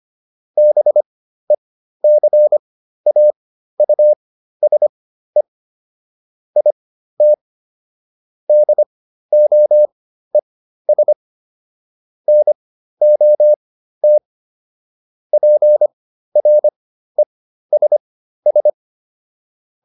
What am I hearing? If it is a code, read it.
Morse code, heard as a keyed tone, about 25 wpm, spaced as for 13 wpm, 600 Hz. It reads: BECAUSE IT DOES NOT PRESS